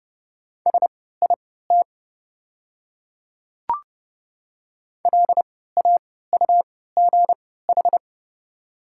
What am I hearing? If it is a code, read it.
Morse code, heard as a keyed tone, about 30 wpm, spaced as for 17 wpm, 690 Hz. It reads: SIT LAUGH
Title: 3x Character Spacing - Top 500 Words - 30wpm